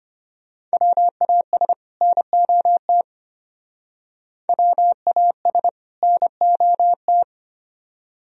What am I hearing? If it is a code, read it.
Morse code, heard as a keyed tone, about 32 wpm, 700 Hz. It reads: WAS NOT WAS NOT